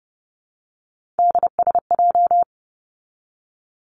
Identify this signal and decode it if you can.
Morse code, heard as a keyed tone, about 30 wpm, 700 Hz. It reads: DSJ
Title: Three Challenging Letter-Number Combinations 30wpm